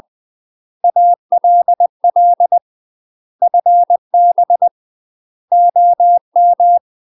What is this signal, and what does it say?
Morse code, heard as a keyed tone, about 20 wpm, 700 Hz. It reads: ALL FB OM
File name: MASTER QSOs TRAINING/20WPM